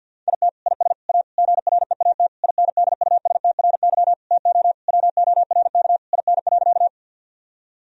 Morse code, heard as a keyed tone, about 50 wpm, 700 Hz. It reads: IT IS A GREAT INDUSTRY TO WORK IN.
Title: Sentences from Top 1000 Words - Review Part 4 - 50wpm